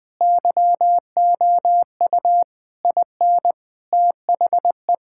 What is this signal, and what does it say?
Morse code, heard as a keyed tone, about 20 wpm, 700 Hz. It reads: YOU IN THE
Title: Sentences from Top 1000 Words - Review Part 6 - 20wpm